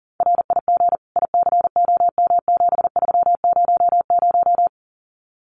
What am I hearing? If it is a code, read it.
Morse code, heard as a keyed tone, about 38 wpm, 700 Hz. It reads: RIG ICOM7300
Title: Common QSO Phrases Encore 40wpm